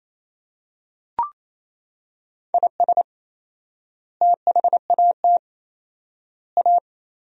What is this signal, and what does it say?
Morse code, heard as a keyed tone, about 28 wpm, 700 Hz. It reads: IS THAT A
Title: 4x Word-Spacing - Sentences from Top 500 Words - 28wpm